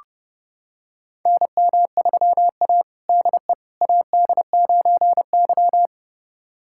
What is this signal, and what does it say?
Morse code, heard as a keyed tone, about 30 wpm, 700 Hz. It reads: NM3A DE AD9Y
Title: USA Callsign DE Callsign - 4 Characters - 30wpm